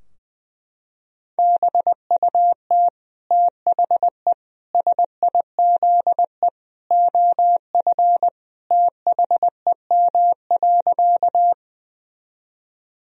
Morse code, heard as a keyed tone, about 20 wpm, 700 Hz. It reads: BUT THE SIZE OF THEM.
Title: Sentences from Top 1000 Words - Review Part 2 - 20wpm